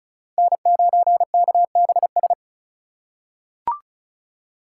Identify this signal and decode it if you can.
Morse code, heard as a keyed tone, about 36 wpm, 690 Hz. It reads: N9KBS
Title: USA Call Signs - 5 Characters - 3x Character Spacing - 35wpm